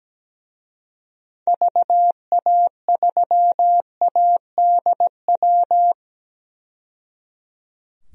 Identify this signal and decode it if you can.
Morse code, heard as a keyed tone, about 17 wpm, 700 Hz. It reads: VA3ADW
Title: DXCC-100 Call Signs - 6 Characters - 17wpm